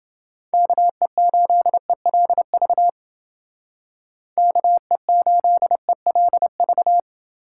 Morse code, heard as a keyed tone, about 32 wpm, 700 Hz. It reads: KE8ELV KE8ELV